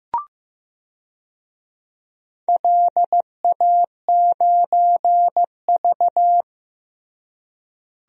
Morse code, heard as a keyed tone, about 15 wpm, 700 Hz. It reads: LA9V
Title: DXCC-100 Call Signs - 4 Characters - Repeated 3x - 15wpm